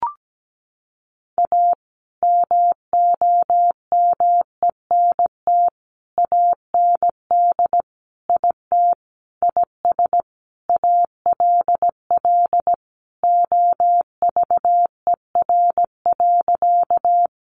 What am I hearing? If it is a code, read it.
Morse code, heard as a keyed tone, about 17 wpm, 700 Hz. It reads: A MOMENT AND IT IS ALL OVER.
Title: Sentences from Top 1000 Words 17wpm